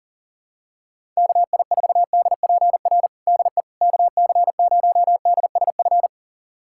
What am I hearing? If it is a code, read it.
Morse code, heard as a keyed tone, about 40 wpm, 700 Hz. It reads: KI4DPR DE KC0DSF
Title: USA Callsign DE Callsign 40wpm